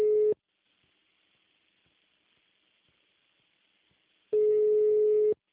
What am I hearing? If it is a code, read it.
Ringback tone (Europe, 425 Hz).
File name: out- -2003- - - .69